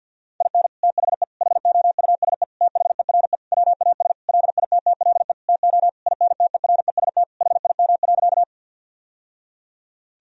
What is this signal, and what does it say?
Morse code, heard as a keyed tone, about 50 wpm, 700 Hz. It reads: IN THE HOUSE THERE WAS LITTLE TO INTEREST HIM.
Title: Sentences from Top 1000 Words - Review Part 2 - 50wpm